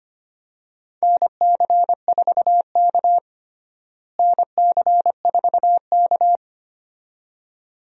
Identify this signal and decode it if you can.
Morse code, heard as a keyed tone, about 25 wpm, 700 Hz. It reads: NC4K NC4K
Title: USA Call Signs - 4 Characters - Repeated 2x - 25wpm